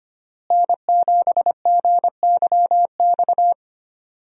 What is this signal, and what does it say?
Morse code, heard as a keyed tone, about 25 wpm, 700 Hz. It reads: N7GYX
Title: USA Call Signs - 5 Characters - Repeated 3x - 25wpm